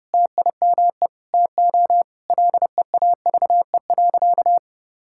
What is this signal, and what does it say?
Morse code, heard as a keyed tone, about 30 wpm, 700 Hz. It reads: TIME TO LEAVE.
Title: Sentences from Top 1000 Words - Review Part 5 - 30wpm